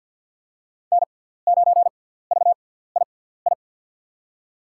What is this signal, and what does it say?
Morse code, heard as a keyed tone, about 50 wpm, spaced as for 17 wpm, 700 Hz. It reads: N9VII
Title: USA Call Signs - 5 Characters - 6x Character Spacing - 50wpm